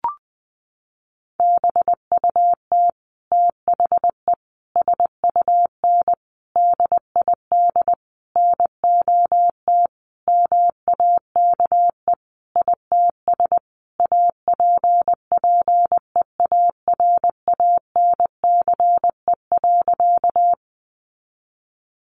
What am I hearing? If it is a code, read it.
Morse code, heard as a keyed tone, about 20 wpm, 700 Hz. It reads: BUT THE SUN DID NOT MAKE ITS APPEARANCE.
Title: Sentences from Top 1100 Words 20wpm